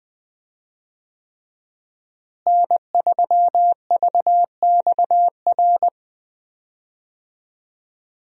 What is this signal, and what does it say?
Morse code, heard as a keyed tone, about 20 wpm, 700 Hz. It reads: N3VXR
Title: USA Call Signs - 5 Characters - Repeated 3x - 20wpm